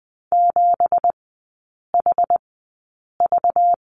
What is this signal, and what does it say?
Morse code, heard as a keyed tone, about 20 wpm, spaced as for 8 wpm, 700 Hz. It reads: 7HV
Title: 191 Lesson 31e - Letter v - Three Characters 20wpm